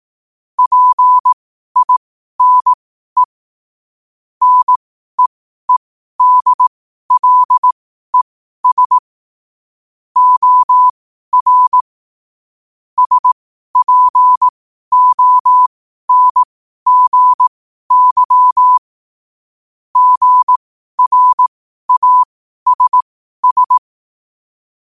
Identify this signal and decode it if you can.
Morse code, heard as a keyed tone, about 18 wpm, spaced as for 12 wpm, 1000 Hz. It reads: PINE NEEDLES OR SPONGY GRASS
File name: qotdfiles / 12wpm